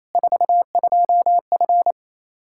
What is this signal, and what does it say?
Morse code, heard as a keyed tone, about 27 wpm, 700 Hz. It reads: 42F
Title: Practice QSO - Exchange 2 - Info - 28wpm